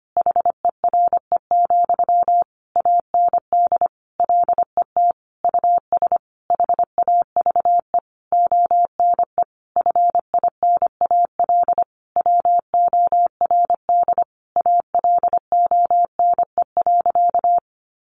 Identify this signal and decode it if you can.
Morse code, heard as a keyed tone, about 25 wpm, 700 Hz. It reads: HERE, AND LET US HAVE ONE FINAL WORD ALONE.
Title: Sentences from Top 1000 Words 25wpm